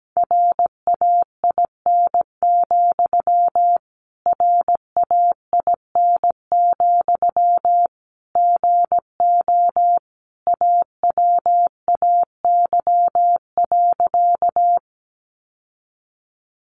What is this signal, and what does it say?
Morse code, heard as a keyed tone, about 17 wpm, 700 Hz. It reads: RAIN, RAIN, GO AWAY.
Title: Sentences from Top 600 Words 17wpm